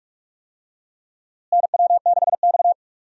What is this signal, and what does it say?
Morse code, heard as a keyed tone, about 46 wpm, 700 Hz. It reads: NW6X
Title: USA Call Signs - 4 Characters - Repeated 2x - 45wpm